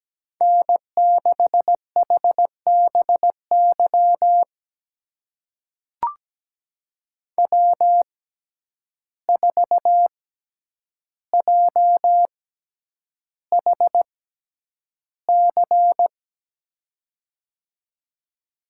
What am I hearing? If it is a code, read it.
Morse code, heard as a keyed tone, about 17 wpm, spaced as for 9 wpm, 700 Hz. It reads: N6HBY W4JHC
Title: USA Call Signs - 5 Characters - 6x Character Spacing - 17wpm